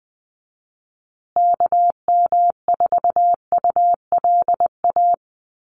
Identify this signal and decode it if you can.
Morse code, heard as a keyed tone, about 20 wpm, 700 Hz. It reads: KM4ULA
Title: USA Call Signs Repeated 2x 20wpm